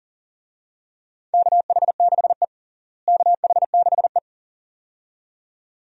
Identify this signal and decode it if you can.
Morse code, heard as a keyed tone, about 40 wpm, 700 Hz. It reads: KH6E KH6E
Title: DXCC-100 Call Signs - 4 Characters - Repeated 2x - 40wpm